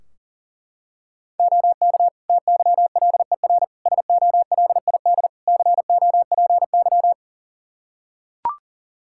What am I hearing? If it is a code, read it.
Morse code, heard as a keyed tone, about 40 wpm, 700 Hz. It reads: OK TYLER SOLID COPY
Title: Practice QSO - Exchange 3 - More Info - 40wpm